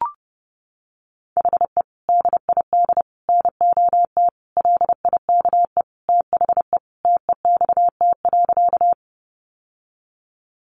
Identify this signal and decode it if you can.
Morse code, heard as a keyed tone, about 30 wpm, 700 Hz. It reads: HE DID NOT LIKE THE TEXT.
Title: Sentences from Top 1200 Words 30wpm